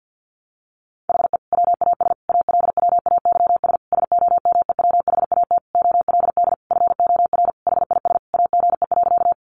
Morse code, heard as a keyed tone, about 50 wpm, 700 Hz. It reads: HE WAS ALWAYS SOMEWHAT OLD FOR HIS AGE.
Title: Sentences from Top 1300 Words 50wpm